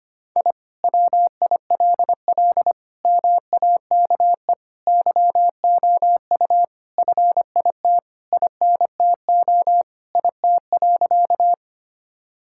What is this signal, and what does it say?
Morse code, heard as a keyed tone, about 25 wpm, 700 Hz. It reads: I WILL MAKE YOU FIT INTO IT.
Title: Sentences from Top 1000 Words - Review Part 5 - 25wpm